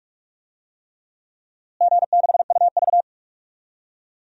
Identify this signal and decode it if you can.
Morse code, heard as a keyed tone, about 42 wpm, 700 Hz. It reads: G6UV
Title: DXCC-100 Call Signs - 4 Characters - Repeated 3x - 45wpm